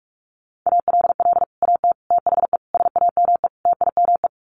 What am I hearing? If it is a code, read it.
Morse code, heard as a keyed tone, about 44 wpm, 700 Hz. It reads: ALL AT THE SAME TIME
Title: Sets of 5 Words Encore2 45wpm